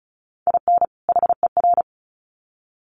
Morse code, heard as a keyed tone, about 34 wpm, 700 Hz. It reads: IN HER
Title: Sets of 2 Words 35wpm